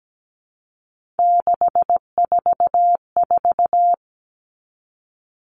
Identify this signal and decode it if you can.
Morse code, heard as a keyed tone, about 17 wpm, 700 Hz. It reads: 644
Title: All Cut Numbers - 3-Digit 17wpm